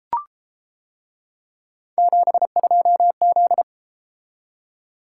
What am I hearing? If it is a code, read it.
Morse code, heard as a keyed tone, about 32 wpm, 700 Hz. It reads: 72Z